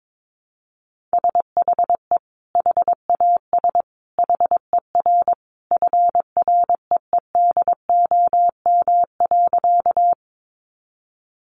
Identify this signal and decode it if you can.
Morse code, heard as a keyed tone, about 22 wpm, 700 Hz. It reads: SHE HAS HER FREEDOM.
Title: Sentences from Top 800 Words 22wpm